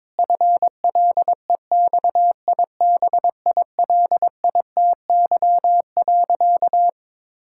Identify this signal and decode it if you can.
Morse code, heard as a keyed tone, about 21 wpm, 700 Hz. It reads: FLEXIBILITY.
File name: Difficult Sentences with Hints - 22wpm